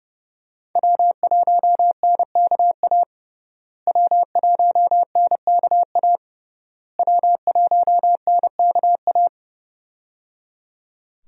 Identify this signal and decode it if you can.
Morse code, heard as a keyed tone, about 30 wpm, 700 Hz. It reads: W1NKA W1NKA W1NKA
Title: USA Call Signs - 5 Characters - Repeated 3x - 30wpm